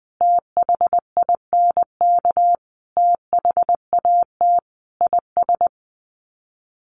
Morse code, heard as a keyed tone, about 20 wpm, 700 Hz. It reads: THINK THAT IS
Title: Sets of 5 Words Encore2 20wpm